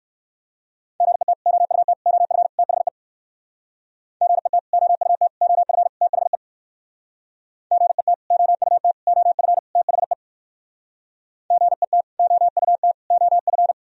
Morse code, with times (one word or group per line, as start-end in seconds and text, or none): 1.0-1.3 GET
1.5-1.9 OUT
2.1-2.5 OF
2.6-2.9 THE
4.2-4.6 GET
4.7-5.3 OUT
5.4-5.9 OF
6.0-6.4 THE
7.7-8.1 GET
8.3-8.9 OUT
9.1-9.6 OF
9.8-10.1 THE
11.5-12.0 GET
12.2-12.9 OUT
13.1-13.7 OF